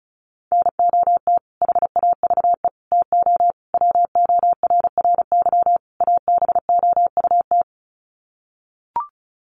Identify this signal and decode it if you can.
Morse code, heard as a keyed tone, about 34 wpm, 700 Hz. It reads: NOT HAVE TO WORRY ABOUT
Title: Sets of 5 Words 35wpm